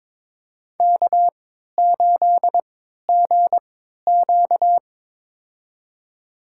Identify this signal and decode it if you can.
Morse code, heard as a keyed tone, about 22 wpm, spaced as for 13 wpm, 700 Hz. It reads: K8GQ
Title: USA Call Signs - 4 Characters - 3x Character Spacing - 22wpm